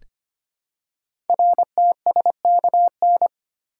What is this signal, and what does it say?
Morse code, heard as a keyed tone, about 25 wpm, 700 Hz. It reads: RTSKN
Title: Code Groups - 5 Letters - 25wpm